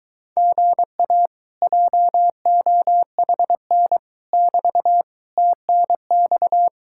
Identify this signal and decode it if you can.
Morse code, heard as a keyed tone, about 23 wpm, 700 Hz. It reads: GA JOHN <BT> TNX